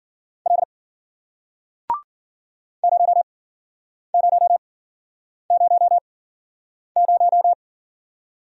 Morse code, heard as a keyed tone, about 49 wpm, spaced as for 9 wpm, 700 Hz. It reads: L 0000
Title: Speed-Racing Single Letter-Number 40wpm